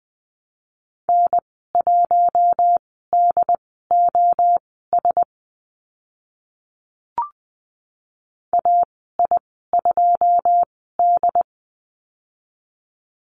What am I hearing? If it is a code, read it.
Morse code, heard as a keyed tone, about 20 wpm, spaced as for 14 wpm, 700 Hz. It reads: N1DOS AI2D
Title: 082 Lesson 14g - Number 2 - Call Signs 20wpm